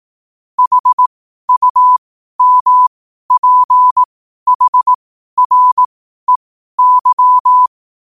Morse code, heard as a keyed tone, about 18 wpm, spaced as for 12 wpm, 1000 Hz. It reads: HUMPHREY
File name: qotdfiles / 12wpm